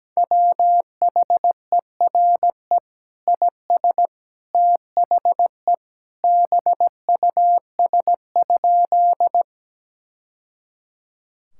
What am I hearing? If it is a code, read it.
Morse code, heard as a keyed tone, about 17 wpm, 700 Hz. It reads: WHERE IS THE BUS?
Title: Sentences from Top 1000 Words - Review Part 6 - 17wpm